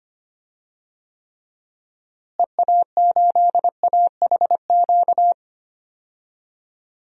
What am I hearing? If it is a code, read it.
Morse code, heard as a keyed tone, about 25 wpm, 700 Hz. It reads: EA8AHQ
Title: DXCC-100 Call Signs - 6 Characters - Repeated 3x - 25wpm